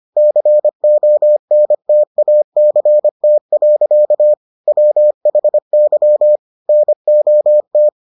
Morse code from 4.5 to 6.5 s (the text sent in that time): WHY